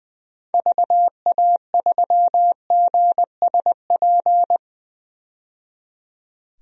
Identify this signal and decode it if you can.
Morse code, heard as a keyed tone, about 20 wpm, 700 Hz. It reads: VA3GSP
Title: DXCC-100 Call Signs - 6 Characters - 20wpm